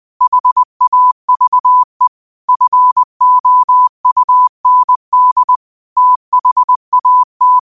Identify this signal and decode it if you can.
Morse code, heard as a keyed tone, about 20 wpm, 1000 Hz. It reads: HAVE FOUND THAT
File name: qotdfiles / 20wpm